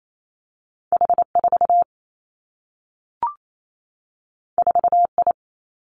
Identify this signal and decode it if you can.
Morse code, heard as a keyed tone, about 28 wpm, 700 Hz. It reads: H 4 4 I
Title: Two Challenging Letter-Number Combinations 28wpm